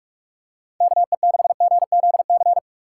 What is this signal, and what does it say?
Morse code, heard as a keyed tone, about 46 wpm, 700 Hz. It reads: KE6GZC
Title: USA Call Signs - 6 Characters - 4x Character Spacing - 45wpm